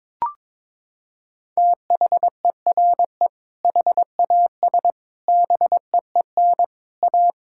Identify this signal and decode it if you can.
Morse code, heard as a keyed tone, about 21 wpm, 700 Hz. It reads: THERE HAS BEEN A
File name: Sentences with 7-Word Phrases - 22wpm